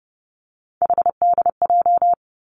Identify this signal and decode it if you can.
Morse code, heard as a keyed tone, about 30 wpm, 700 Hz. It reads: HDJ
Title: Three Challenging Letter-Number Combinations 30wpm